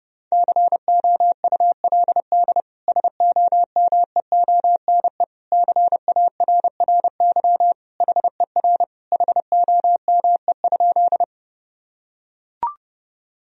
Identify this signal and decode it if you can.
Morse code, heard as a keyed tone, about 32 wpm, 700 Hz. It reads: COULD SOMEONE CARRY HER HOME?
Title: Sentences from Top 1000 Words - Review Part 4 - 30wpm